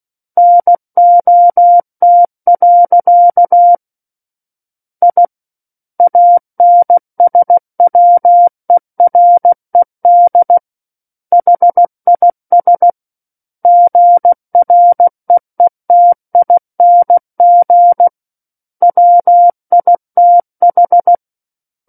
Morse code, heard as a keyed tone, about 16 wpm, 700 Hz. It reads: NOT. I ANSWERED HIS GREETING WITH
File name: Book One / 07-Chapter 7 - 16 WPM